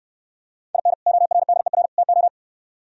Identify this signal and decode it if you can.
Morse code, heard as a keyed tone, about 68 wpm, 700 Hz. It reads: IN ORDER TO